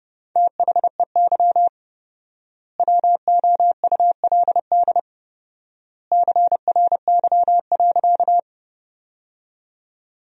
Morse code, heard as a keyed tone, about 32 wpm, 700 Hz. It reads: THEY WOULD CRY.
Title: 4x Word-Spacing - Sentences from Top 500 Words - 30wpm